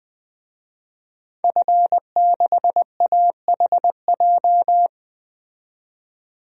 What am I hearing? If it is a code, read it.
Morse code, heard as a keyed tone, about 20 wpm, 700 Hz. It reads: F6AHJ
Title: DXCC-100 Call Signs - 5 Characters - 20wpm